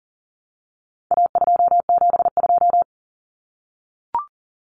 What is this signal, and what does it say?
Morse code, heard as a keyed tone, about 40 wpm, 700 Hz. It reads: A272
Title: Common Cut Numbers - 4-Digit 40wpm